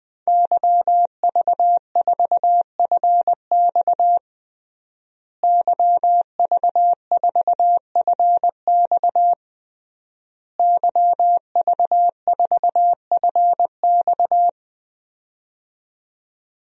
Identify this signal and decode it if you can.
Morse code, heard as a keyed tone, about 20 wpm, 700 Hz. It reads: YV4FX YV4FX YV4FX
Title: DXCC-100 Call Signs - 5 Characters - Repeated 3x - 20wpm